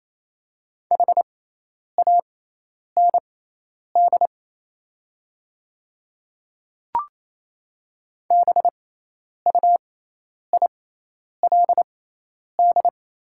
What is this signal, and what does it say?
Morse code, heard as a keyed tone, about 28 wpm, spaced as for 10 wpm, 700 Hz. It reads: HAND BUILD
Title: 6x Character Spacing - Top 200 Words - 28wpm